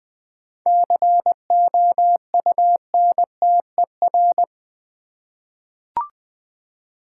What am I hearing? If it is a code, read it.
Morse code, heard as a keyed tone, about 20 wpm, 700 Hz. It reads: COUNTER
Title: Seven Letter Words - 20wpm